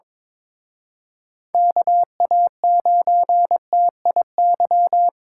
Morse code, heard as a keyed tone, about 27 wpm, 700 Hz. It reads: KA9TIY
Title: Speed Racing - USA 2x3 Call Signs 22wpm